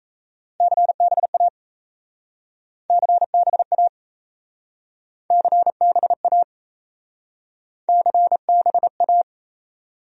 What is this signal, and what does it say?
Morse code, heard as a keyed tone, about 37 wpm, 700 Hz. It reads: CBA CBA CBA CBA